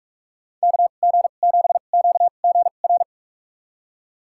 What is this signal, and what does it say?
Morse code, heard as a keyed tone, about 42 wpm, spaced as for 31 wpm, 700 Hz. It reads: KG7QGR
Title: USA Call Signs - 6 Characters - 2x Character Spacing - 45wpm